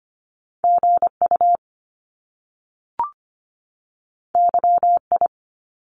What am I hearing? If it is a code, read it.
Morse code, heard as a keyed tone, about 25 wpm, 700 Hz. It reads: GU YI
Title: Top Two Letter Combinations in Words 25wpm